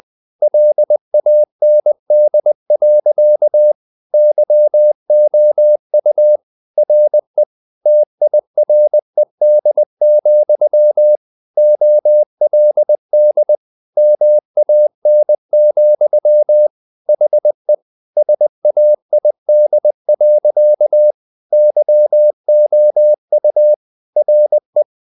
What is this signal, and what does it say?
Morse code, heard as a keyed tone, about 20 wpm, 600 Hz. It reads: LAND. YOU RE TIRED, OLD MAN, HE SAID. YOU RE
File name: the old man and the sea part 2 20wpm